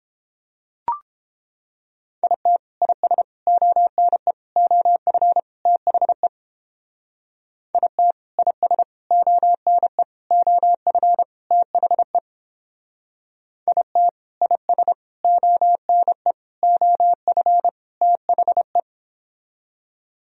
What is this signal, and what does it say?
Morse code, heard as a keyed tone, about 31 wpm, 700 Hz. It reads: IT IS ONE OF THE IT IS ONE OF THE IT IS ONE OF THE